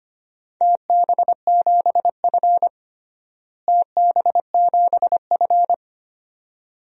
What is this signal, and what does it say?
Morse code, heard as a keyed tone, about 25 wpm, 700 Hz. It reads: TB7F TB7F
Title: DXCC-100 Call Signs - 4 Characters - Repeated 2x - 25wpm